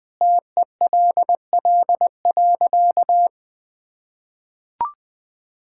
Morse code, heard as a keyed tone, about 20 wpm, 700 Hz. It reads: TELL.
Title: Sentences with 2-Word Phrases - 20wpm